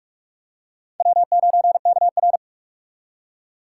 Morse code, heard as a keyed tone, about 44 wpm, 700 Hz. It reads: W9KR